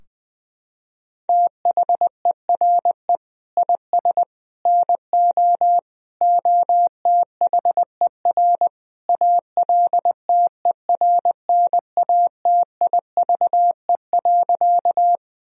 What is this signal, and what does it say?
Morse code, heard as a keyed tone, about 20 wpm, 700 Hz. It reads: THERE IS NO OTHER ALTERNATIVE.
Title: Sentences from Top 1000 Words - Review Part 4 - 20wpm